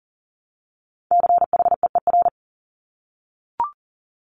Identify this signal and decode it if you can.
Morse code, heard as a keyed tone, about 37 wpm, 700 Hz. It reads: CHEER
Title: Multi-letter Phonemes Repeated 3x 40wpm